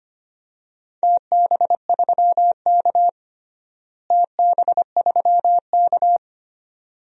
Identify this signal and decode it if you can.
Morse code, heard as a keyed tone, about 25 wpm, 700 Hz. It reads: TB3K TB3K
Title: DXCC-100 Call Signs - 4 Characters - Repeated 2x - 25wpm